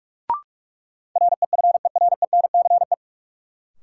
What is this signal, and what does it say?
Morse code, heard as a keyed tone, about 42 wpm, 700 Hz. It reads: REFERENCE
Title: Sentences from Top 1200 Words - New Word Drill - 45wpm